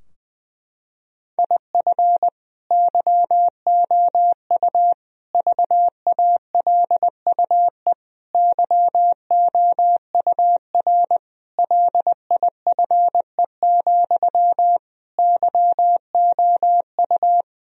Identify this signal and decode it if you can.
Morse code, heard as a keyed tone, about 20 wpm, 700 Hz. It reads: IF YOU VALUE YOUR LIFE, YOU
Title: Sentences from Top 1000 Words - Review Part 3 - 20wpm